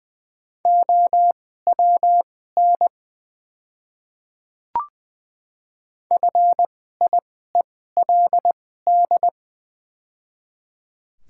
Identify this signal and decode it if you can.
Morse code, heard as a keyed tone, about 20 wpm, spaced as for 14 wpm, 690 Hz. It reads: OWN FIELD
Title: 2x Character Spacing - Top 500 Words - 20wpm